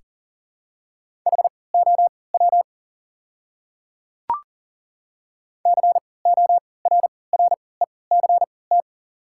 Morse code, heard as a keyed tone, about 37 wpm, spaced as for 22 wpm, 690 Hz. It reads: HOW CORRECT
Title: 3x Character Spacing - Top 500 Words - 40wpm